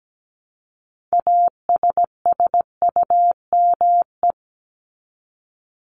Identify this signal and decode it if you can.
Morse code, heard as a keyed tone, about 17 wpm, 700 Hz. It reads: ASSUME
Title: Top 1251-1500 Words 17wpm